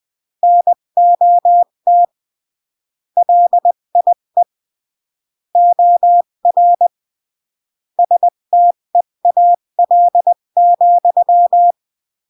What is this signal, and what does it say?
Morse code, heard as a keyed tone, about 20 wpm, 700 Hz. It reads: NOT LIE OR STEAL,
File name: CW / jodorowsky / 82 commandments 20-15